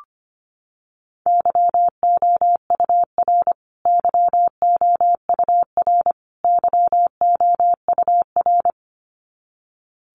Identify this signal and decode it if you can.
Morse code, heard as a keyed tone, about 25 wpm, 700 Hz. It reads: YOUR YOUR YOUR
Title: Top 300 Words Repeated 25wpm